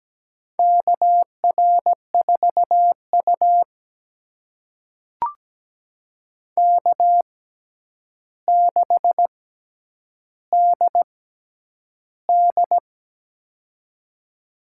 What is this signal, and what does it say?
Morse code, heard as a keyed tone, about 17 wpm, spaced as for 9 wpm, 700 Hz. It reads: KR4U K6DD
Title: USA Call Signs - 4 Characters - 6x Character Spacing - 17wpm